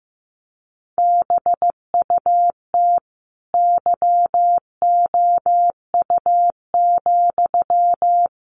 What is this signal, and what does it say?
Morse code, heard as a keyed tone, about 15 wpm, 700 Hz. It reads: BUT YOU,
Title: A Sound Of Thunder 15wpm